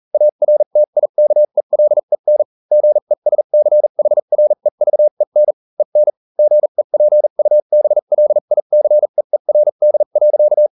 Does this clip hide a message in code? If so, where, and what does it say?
Yes: Morse code from 0.1 to 10.7 s, heard as a keyed tone, about 40 wpm, spaced as for 35 wpm, 600 Hz. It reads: ARTIKELEN GESCHREVEN EN GEPUBLICEERD.